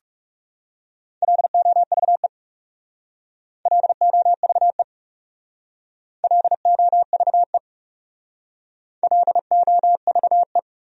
Morse code, heard as a keyed tone, about 36 wpm, 700 Hz. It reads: LOVE LOVE LOVE LOVE
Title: Speed-Racing Top 500 Words 30wpm